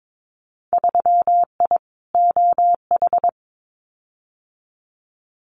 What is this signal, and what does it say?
Morse code, heard as a keyed tone, about 22 wpm, 700 Hz. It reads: 3I OH
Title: Winter Field Day Exchanges Encore 22wpm